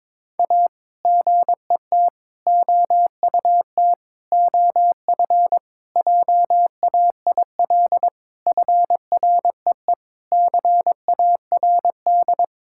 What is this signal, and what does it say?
Morse code, heard as a keyed tone, about 21 wpm, 700 Hz. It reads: A GET OUT OF JAIL FREE CARD
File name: practice / 22wpm / Sets of 7 Words - 22wpm